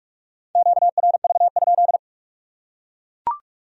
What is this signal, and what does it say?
Morse code, heard as a keyed tone, about 41 wpm, 700 Hz. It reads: QRV?